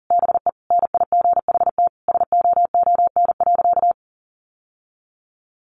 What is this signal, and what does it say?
Morse code, heard as a keyed tone, about 40 wpm, 700 Hz. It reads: BE NIGHT SOON.
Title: Sentences from Top 1300 Words 40wpm